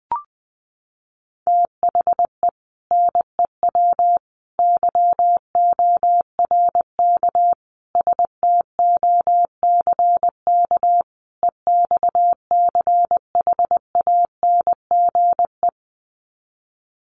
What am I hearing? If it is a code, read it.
Morse code, heard as a keyed tone, about 20 wpm, 700 Hz. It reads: THE NEW YORK STOCK EXCHANGE
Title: Sets of 5 Words 20wpm